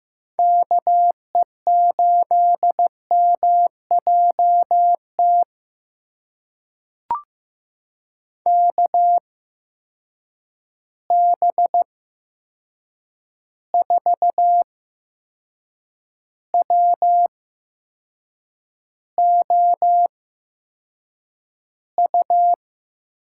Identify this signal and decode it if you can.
Morse code, heard as a keyed tone, about 15 wpm, spaced as for 6 wpm, 700 Hz. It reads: KE8MJT KB4WOU
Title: USA Call Signs - 6 Characters - 8x Character Spacing - 15wpm